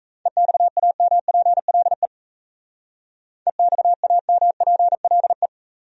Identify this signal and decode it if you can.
Morse code, heard as a keyed tone, about 34 wpm, 700 Hz. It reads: EXAMPLE EXAMPLE
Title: Speed-Racing Top 500 Words 28wpm